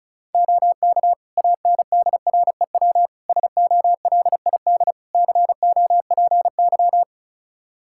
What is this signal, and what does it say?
Morse code, heard as a keyed tone, about 36 wpm, 700 Hz. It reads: OK ANDREW SOLID COPY
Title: Practice QSO - Exchange 3 - More Info - 35wpm